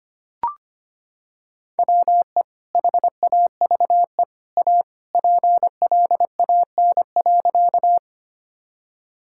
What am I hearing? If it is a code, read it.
Morse code, heard as a keyed tone, about 25 wpm, 700 Hz. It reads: WE HAVE A PLAN.
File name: practice / 25wpm / Sentences with 2-Word Phrases - 25wpm